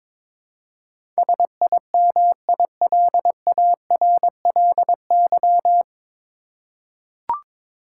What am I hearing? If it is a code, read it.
Morse code, heard as a keyed tone, about 22 wpm, 700 Hz. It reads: SIMILARLY
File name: 9 Letter Words - First 3 Letters Given - 22wpm